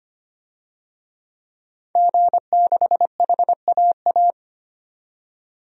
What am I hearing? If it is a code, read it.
Morse code, heard as a keyed tone, about 25 wpm, 700 Hz. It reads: G6HAA
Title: DXCC-100 Call Signs - 5 Characters - 25wpm